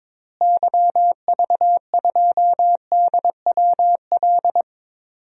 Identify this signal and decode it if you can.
Morse code, heard as a keyed tone, about 22 wpm, 700 Hz. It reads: YV2DWL
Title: DXCC-100 Call Signs - 6 Characters - Repeated 3x - 22wpm